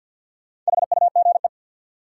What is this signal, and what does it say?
Morse code, heard as a keyed tone, about 48 wpm, 700 Hz. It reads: HUGE
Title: Sentences from Top 1100 Words - New Word Drill Alt - 50wpm